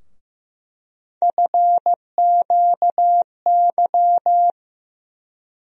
Morse code, heard as a keyed tone, about 15 wpm, 700 Hz. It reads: FQY
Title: USA Call Sign Postfixes - 3 Letters 15wpm